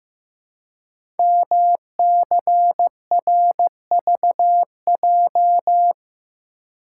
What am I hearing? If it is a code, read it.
Morse code, heard as a keyed tone, about 15 wpm, 700 Hz. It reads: MCRVJ
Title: Code Groups - 5 Letters - 15wpm